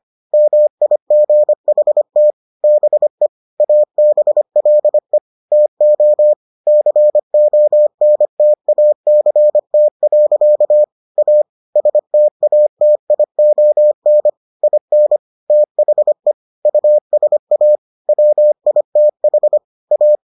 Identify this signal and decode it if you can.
Morse code, heard as a keyed tone, about 25 wpm, 600 Hz. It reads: MIGHT BE ABLE TO CONTACT. A STATION IN THE USA WITH A